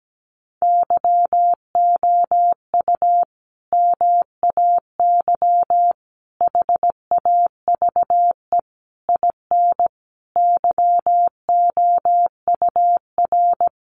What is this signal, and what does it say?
Morse code, heard as a keyed tone, about 17 wpm, 700 Hz. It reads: YOU MAY HAVE IN YOUR
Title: Sentences from Top 1400 Words 17wpm